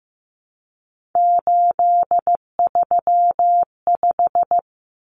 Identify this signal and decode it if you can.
Morse code, heard as a keyed tone, about 15 wpm, 700 Hz. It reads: 835
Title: Numbers - 3-Digit 15wpm